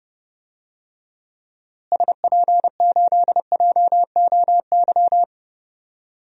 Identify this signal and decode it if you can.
Morse code, heard as a keyed tone, about 32 wpm, 700 Hz. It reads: SP8JOY
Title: DXCC-100 Call Signs - 6 Characters - Repeated 2x - 30wpm